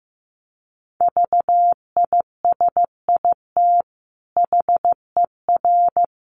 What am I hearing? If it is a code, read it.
Morse code, heard as a keyed tone, about 15 wpm, 700 Hz. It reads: VISIT HER
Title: Sentences from Top 900 Words 15wpm